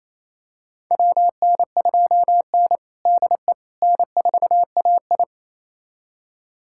Morse code, heard as a keyed tone, about 27 wpm, 700 Hz. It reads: WN2N DE N4AI
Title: USA Callsign DE Callsign - 4 Characters - 28wpm